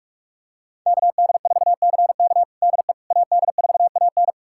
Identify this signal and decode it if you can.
Morse code, heard as a keyed tone, about 43 wpm, 700 Hz. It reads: KD4CK DE AD4AN
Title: USA Callsign DE Callsign - 5 Characters - 45wpm